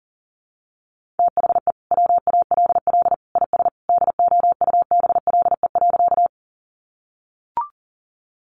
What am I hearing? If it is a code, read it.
Morse code, heard as a keyed tone, about 40 wpm, 700 Hz. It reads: THE WALL IS DOUBLE.
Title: Sentences from Top 1400 Words 40wpm